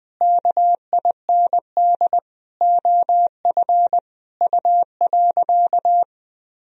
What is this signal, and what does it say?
Morse code, heard as a keyed tone, about 21 wpm, 700 Hz. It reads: KIND OF U.
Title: Sentences from Top 200 Words and Abbreviations 20wpm